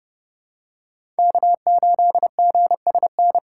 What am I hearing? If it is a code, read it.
Morse code, heard as a keyed tone, about 30 wpm, 700 Hz. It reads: K8GSN
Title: USA Call Signs - 5 Characters - 3x Character Spacing - 30wpm